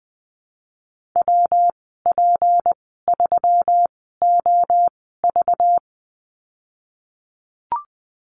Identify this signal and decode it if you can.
Morse code, heard as a keyed tone, about 20 wpm, spaced as for 15 wpm, 700 Hz. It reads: WP3OV
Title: 193 Lesson 31g - Letter v - Call Signs 20wpm